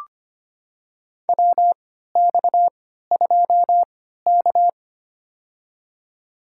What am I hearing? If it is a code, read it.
Morse code, heard as a keyed tone, about 25 wpm, spaced as for 14 wpm, 700 Hz. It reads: WX2K
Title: USA Call Signs - 4 Characters - 3x Character Spacing - 25wpm